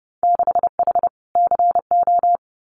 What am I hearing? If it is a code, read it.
Morse code, heard as a keyed tone, about 30 wpm, 700 Hz. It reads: 6H CO
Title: Winter Field Day Exchanges Encore 30wpm